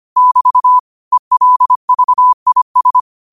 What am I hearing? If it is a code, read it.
Morse code, heard as a keyed tone, about 25 wpm, 1000 Hz. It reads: <BT> ELVIS
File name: qotdfiles / 25wpm